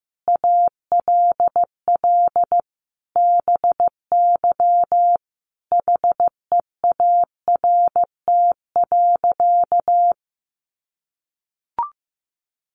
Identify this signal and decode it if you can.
Morse code, heard as a keyed tone, about 14 wpm, 700 Hz. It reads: ALL BY HEART.
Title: Sentences from Top 1200 Words 15wpm